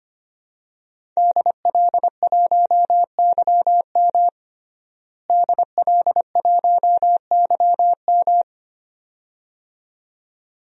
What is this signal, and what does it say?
Morse code, heard as a keyed tone, about 25 wpm, 700 Hz. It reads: DL1YM DL1YM
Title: DXCC-100 Call Signs - 5 Characters - Repeated 2x - 25wpm